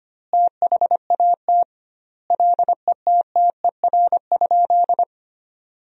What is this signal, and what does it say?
Morse code, heard as a keyed tone, about 25 wpm, 700 Hz. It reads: THAT LETTER?
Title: 2x Word-Spacing - Sentences from Top 300 Words - 25wpm